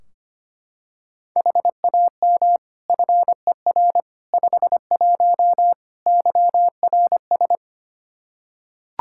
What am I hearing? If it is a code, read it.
Morse code, heard as a keyed tone, about 25 wpm, 700 Hz. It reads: HAM FER 51 YRS
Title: Practice QSO - Exchange 3 - More Info - 25wpm